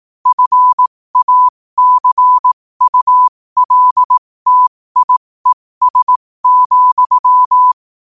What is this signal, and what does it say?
Morse code, heard as a keyed tone, about 18 wpm, spaced as for 15 wpm, 1000 Hz. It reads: FACULTIES,
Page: New quotes every day in morse code at 15 Words per minute.